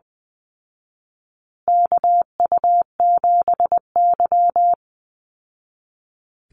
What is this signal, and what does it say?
Morse code, heard as a keyed tone, about 20 wpm, 700 Hz. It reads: KU7Y
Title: Call Signs Repeated 2x - 20wpm